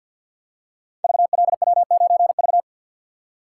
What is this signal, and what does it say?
Morse code, heard as a keyed tone, about 47 wpm, 700 Hz. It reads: VLW9V
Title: Code Groups - 5 Letter:Numbers - 50wpm